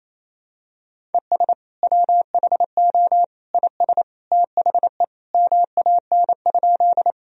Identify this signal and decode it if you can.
Morse code, heard as a keyed tone, about 28 wpm, 700 Hz. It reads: ES WHO IS THE MAN?
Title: Sentences from Top 200 Words and Abbreviations 28wpm